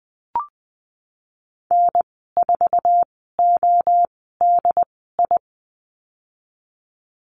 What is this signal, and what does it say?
Morse code, heard as a keyed tone, about 20 wpm, spaced as for 15 wpm, 700 Hz. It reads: N4ODI
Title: 076 Lesson 13g - Letter L - Call Signs 20wpm